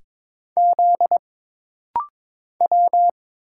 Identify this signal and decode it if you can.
Morse code, heard as a keyed tone, about 22 wpm, 700 Hz. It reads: Z W
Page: This rapid-fire timing is geared to push your ICR (Instant Character Recognition) skills.